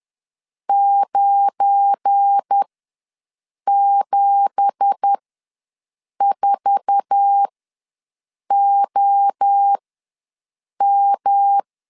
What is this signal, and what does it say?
Morse code, heard as a keyed tone, about 10 wpm, spaced as for 6 wpm, 800 Hz. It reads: 974OM
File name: LEZIONI / 9 2867 ws 10wpm 4215